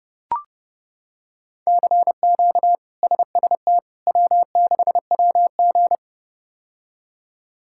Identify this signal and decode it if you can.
Morse code, heard as a keyed tone, about 32 wpm, 700 Hz. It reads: CQ SST W6WG
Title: K1USN SST QSO Practice - 30wpm